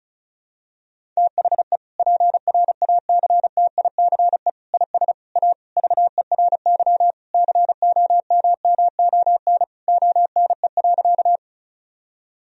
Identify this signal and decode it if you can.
Morse code, heard as a keyed tone, about 36 wpm, 700 Hz. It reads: THE PRACTICE IS A VERY COMMON ONE.
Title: Sentences from Top 1000 Words - Review Part 4 - 35wpm